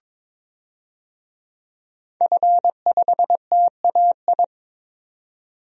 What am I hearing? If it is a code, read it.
Morse code, heard as a keyed tone, about 22 wpm, 700 Hz. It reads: F5TAI
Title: DXCC-100 Call Signs - 5 Characters - 22wpm